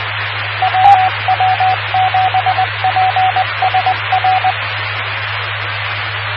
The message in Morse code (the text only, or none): WW7PSR